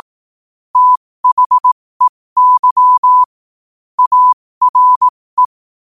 Morse code, heard as a keyed tone, about 18 wpm, spaced as for 15 wpm, 1000 Hz. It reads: THEY ARE
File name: qotdfiles / 15wpm